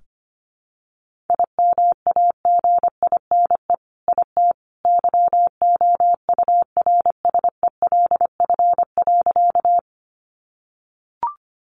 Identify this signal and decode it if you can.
Morse code, heard as a keyed tone, about 25 wpm, 700 Hz. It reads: IMAGINE IT YOURSELF.
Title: Sentences from Top 1400 Words 25wpm